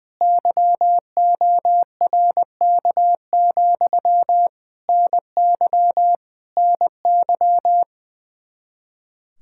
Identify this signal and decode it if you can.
Morse code, heard as a keyed tone, about 21 wpm, 700 Hz. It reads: YORK, NY NY
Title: Practice QSO - Exchange 1 - Beginning - 20wpm